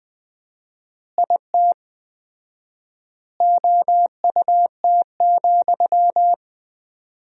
Morse code, heard as a keyed tone, about 20 wpm, 700 Hz. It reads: IT OUT,
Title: 4x Word-Spacing - Sentences from Top 300 Words - 20wpm